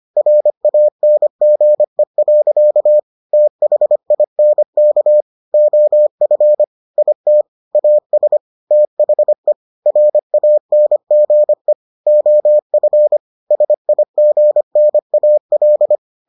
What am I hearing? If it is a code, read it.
Morse code, heard as a keyed tone, about 25 wpm, 600 Hz. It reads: RANGE. THINK OF IT AS THE RANGE OF SIGNAL